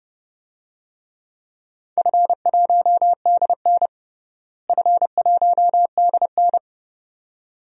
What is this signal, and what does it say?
Morse code, heard as a keyed tone, about 30 wpm, 700 Hz. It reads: F1DN F1DN
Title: DXCC-100 Call Signs - 4 Characters - Repeated 2x - 30wpm